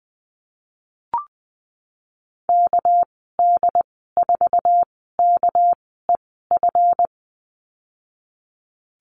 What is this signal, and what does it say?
Morse code, heard as a keyed tone, about 20 wpm, spaced as for 14 wpm, 700 Hz. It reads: KD4KEF
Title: 219 Lesson 35g - Number 0 - Call Signs 20wpm